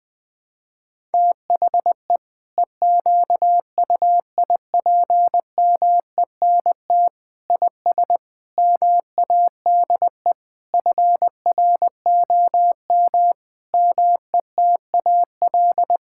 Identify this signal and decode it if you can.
Morse code, heard as a keyed tone, about 20 wpm, 700 Hz. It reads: THE EQUIPMENT IS MADE FROM METAL
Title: Sentences from Top 1000 Words - Review Part 6 - 20wpm